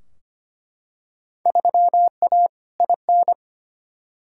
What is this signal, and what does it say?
Morse code, heard as a keyed tone, about 25 wpm, 700 Hz. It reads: 3A IN
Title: Field Day Exchanges - 25wpm